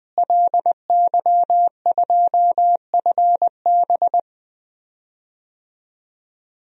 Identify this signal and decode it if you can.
Morse code, heard as a keyed tone, about 21 wpm, 700 Hz. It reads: LY2FB
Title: DXCC-100 Call Signs - 5 Characters - Repeated 2x - 20wpm